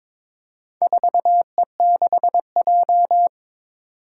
Morse code, heard as a keyed tone, about 22 wpm, 700 Hz. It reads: 4E6J
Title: Four Challenging Letter-Number Combinations 22wpm